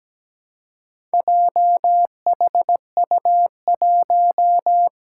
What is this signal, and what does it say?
Morse code, heard as a keyed tone, about 17 wpm, 700 Hz. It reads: JHU1
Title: Four Challenging Letter-Number Combinations 17wpm